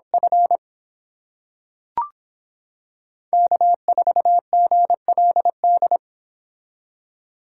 Morse code, heard as a keyed tone, about 26 wpm, 700 Hz. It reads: F K4GLD